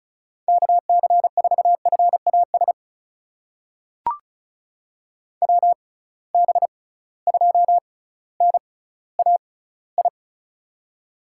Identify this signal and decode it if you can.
Morse code, heard as a keyed tone, about 36 wpm, spaced as for 18 wpm, 700 Hz. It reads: KC4FAS WB2NAI
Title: USA Call Signs - 6 Characters - 6x Character Spacing - 35wpm